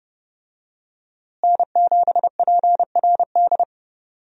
Morse code, heard as a keyed tone, about 30 wpm, 700 Hz. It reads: N7PRD
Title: USA Call Signs - 5 Characters - Repeated 2x - 30wpm